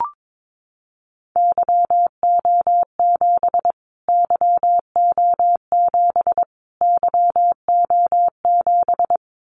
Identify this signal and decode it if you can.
Morse code, heard as a keyed tone, about 22 wpm, 700 Hz. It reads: YO7 YO7 YO7
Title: Callsign Prefixes for Top 25 DXCC - Repeated 3x - 22wpm